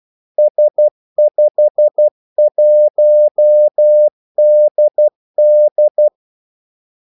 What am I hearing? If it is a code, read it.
Morse code, heard as a keyed tone, about 12 wpm, 600 Hz. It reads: S51DD
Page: 2 ekstra ordmellemrum mellem kaldesignalerne